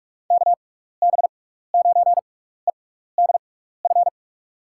Morse code, heard as a keyed tone, about 44 wpm, spaced as for 15 wpm, 700 Hz. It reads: KB9EDF
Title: USA Call Signs - 6 Characters - 6x Character Spacing - 45wpm